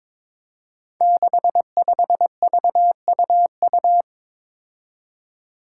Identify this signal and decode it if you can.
Morse code, heard as a keyed tone, about 22 wpm, 700 Hz. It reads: 65VUU
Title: Five Challenging Letter-Number Combinations 22wpm